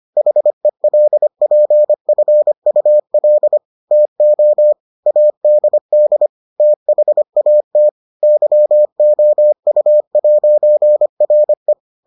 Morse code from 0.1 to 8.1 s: HELPFUL TO ADD THAT

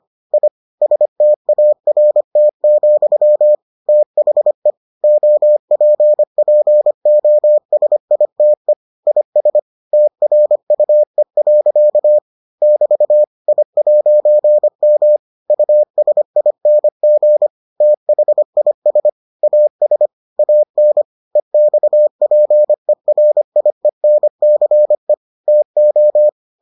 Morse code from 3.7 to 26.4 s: THE OPPOSITE IS TRUE. <BT> I'M USING THIS AS AN EXPERIENCE TO

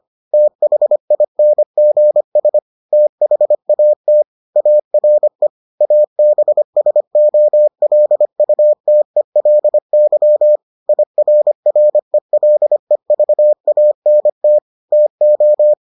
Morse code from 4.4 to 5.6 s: ARE

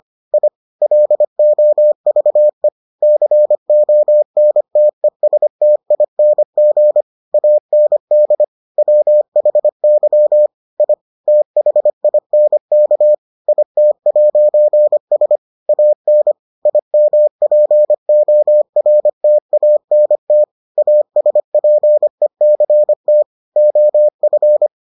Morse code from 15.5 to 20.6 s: AN IMPORTANT

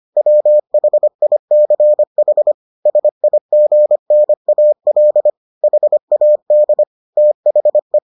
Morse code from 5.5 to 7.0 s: HAD

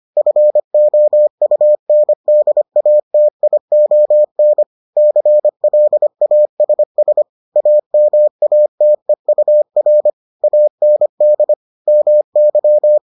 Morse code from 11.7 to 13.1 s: MY

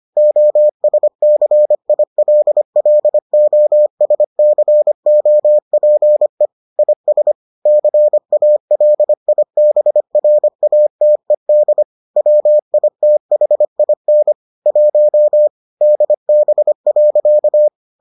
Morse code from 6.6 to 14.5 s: IS CALIBRATED WITHIN

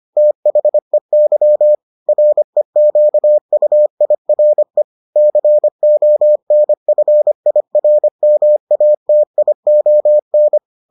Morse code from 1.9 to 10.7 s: REQUIRE CONFIRMATION

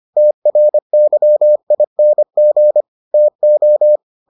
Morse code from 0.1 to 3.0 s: TRYING

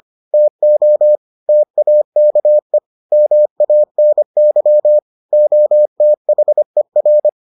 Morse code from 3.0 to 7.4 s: MANY OTHER

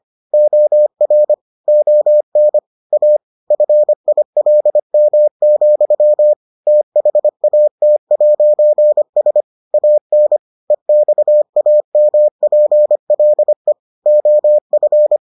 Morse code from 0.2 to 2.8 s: OR ON

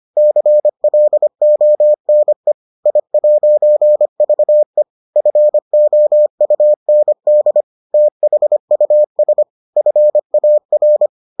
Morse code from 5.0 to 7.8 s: FOUND

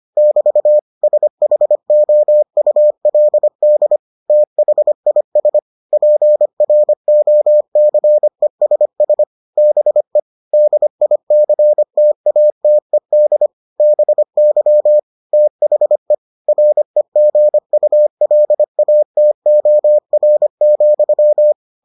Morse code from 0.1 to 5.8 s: <BT> SHOULD THIS

